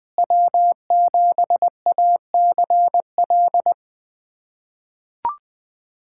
Morse code, heard as a keyed tone, about 20 wpm, 700 Hz. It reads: W7ACL
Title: USA Callsign DE Callsign - 5 Characters - 20wpm